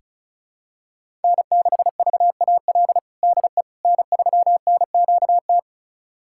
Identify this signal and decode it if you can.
Morse code, heard as a keyed tone, about 36 wpm, 700 Hz. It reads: N6VAL DE N3NQT
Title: USA Callsign DE Callsign - 5 Characters - 35wpm